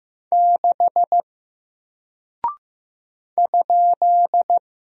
Morse code, heard as a keyed tone, about 15 wpm, 700 Hz. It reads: 6 ?
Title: Single Character (Letter:Number:Punctuation:Prosign) Alt 15wpm